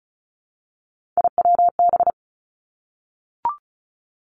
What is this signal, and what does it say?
Morse code, heard as a keyed tone, about 36 wpm, 700 Hz. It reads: IWB
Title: Three Challenging Letter-Number Combinations 35wpm